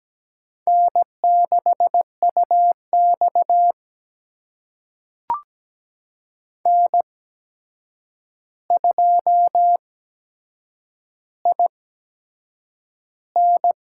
Morse code, heard as a keyed tone, about 17 wpm, spaced as for 7 wpm, 700 Hz. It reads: N6UX N2IN
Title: USA Call Signs - 8x Character Spacing - 17wpm